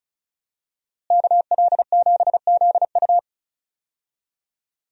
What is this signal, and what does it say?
Morse code, heard as a keyed tone, about 34 wpm, 700 Hz. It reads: KL7ZU
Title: USA Call Signs - 5 Characters - Repeated 2x - 35wpm